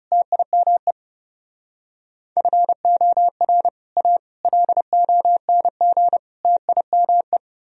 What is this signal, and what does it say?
Morse code, heard as a keyed tone, about 37 wpm, 700 Hz. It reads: TIME FOR A LONG TIME
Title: Speed-Racing - Sets of 4 Words 30wpm